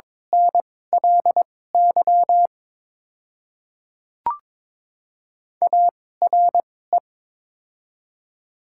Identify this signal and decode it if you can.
Morse code, heard as a keyed tone, about 22 wpm, spaced as for 16 wpm, 700 Hz. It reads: NLY ARE
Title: 2x Character Spacing - Top 200 Words - 22wpm